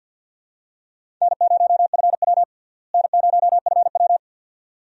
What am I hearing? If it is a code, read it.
Morse code, heard as a keyed tone, about 51 wpm, 700 Hz. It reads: N0FW N0FW
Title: USA Call Signs - 4 Characters - Repeated 2x - 50wpm